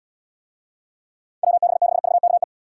Morse code, heard as a keyed tone, about 78 wpm, 700 Hz. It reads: WB6VKE